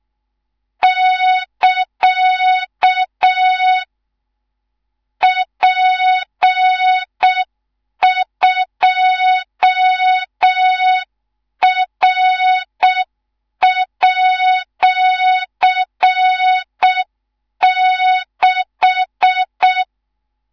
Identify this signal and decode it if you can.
Morse code, heard as a keyed tone, about 6 wpm, 3700 Hz. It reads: <KA> P2R@6
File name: Morze-HA2MN / hang / 19 GYAKORLAT